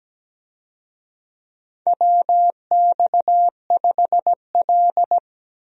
Morse code, heard as a keyed tone, about 17 wpm, 700 Hz. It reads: WX5L
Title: USA Call Signs - 4 Characters - 17wpm